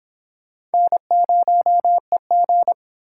Morse code, heard as a keyed tone, about 26 wpm, 700 Hz. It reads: N0EG